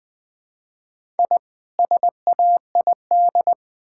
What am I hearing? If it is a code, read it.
Morse code, heard as a keyed tone, about 21 wpm, 700 Hz. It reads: I SAID
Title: Build A Phrase - 2-Word Phrases - 20wpm